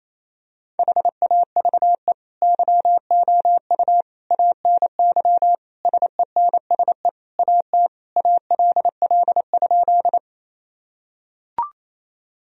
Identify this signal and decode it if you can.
Morse code, heard as a keyed tone, about 28 wpm, 700 Hz. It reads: HAVE YOU ANY SENSE AT ALL?
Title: Sentences from Top 1000 Words - Review Part 4 - 28wpm